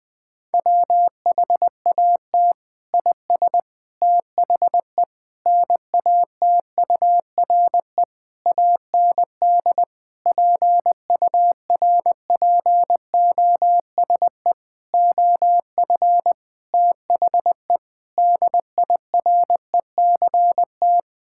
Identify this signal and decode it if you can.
Morse code, heard as a keyed tone, about 20 wpm, 700 Hz. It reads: WHAT IS THE NATURE AND PURPOSE OF THE DIRECT
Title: Sentences from Top 1000 Words - Review Part 5 - 20wpm